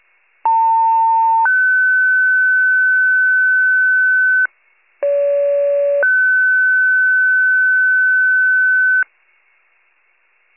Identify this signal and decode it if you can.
Two-tone page, heard as a one-second tone, then a three-second tone, about 900 then 1500 Hz, about 570 then 1500 Hz. It is heard from 0.4 to 9.0 s.
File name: wcalert